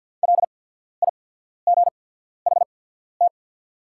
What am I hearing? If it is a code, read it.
Morse code, heard as a keyed tone, about 48 wpm, spaced as for 14 wpm, 700 Hz. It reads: LIGHT
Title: 8x Character Spacing - Top 300 Words - 50wpm